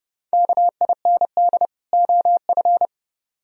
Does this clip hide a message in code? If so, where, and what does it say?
Yes: Morse code from 0.3 to 2.9 s, heard as a keyed tone, about 29 wpm, 700 Hz. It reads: KIND OF